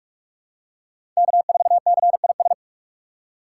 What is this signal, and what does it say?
Morse code, heard as a keyed tone, about 48 wpm, 700 Hz. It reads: K4CIS
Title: USA Call Signs - 3x Character Spacing - 45wpm